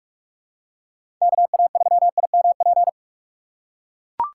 Morse code, heard as a keyed tone, about 41 wpm, 700 Hz. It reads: KA3IMP
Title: USA Call Signs - 6 Characters - 2x Character Spacing - 45wpm